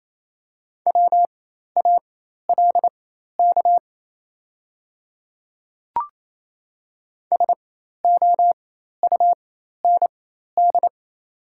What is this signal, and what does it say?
Morse code, heard as a keyed tone, about 27 wpm, spaced as for 13 wpm, 690 Hz. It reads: WALK SOUND
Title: 4x Character Spacing - Top 300 Words - 28wpm